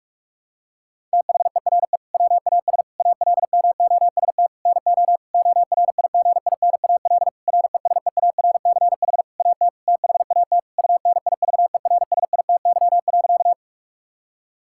Morse code, heard as a keyed tone, about 46 wpm, 700 Hz. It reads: THERE WAS ALMOST NO ORIGINAL RESEARCH AT THAT UNIVERSITY.
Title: Sentences from Top 1000 Words - Review Part 6 - 45wpm